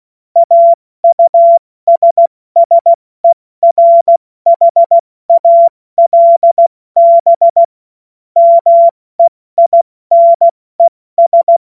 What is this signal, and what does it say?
Morse code, heard as a keyed tone, about 16 wpm, 680 Hz. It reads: AUSSERHALB MEINES
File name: Morsekurs / Witze / Witz 18. Juli 2023 70 F80